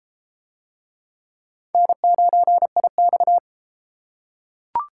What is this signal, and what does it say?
Morse code, heard as a keyed tone, about 29 wpm, 700 Hz. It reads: N9IX